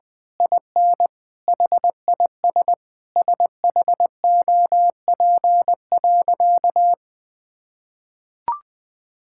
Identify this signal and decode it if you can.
Morse code, heard as a keyed tone, about 20 wpm, 700 Hz. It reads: IN HIS SHOP.
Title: Sentences from Top 1600 Words 20wpm